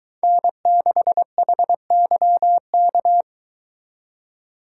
Morse code, heard as a keyed tone, about 23 wpm, 700 Hz. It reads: N6HYK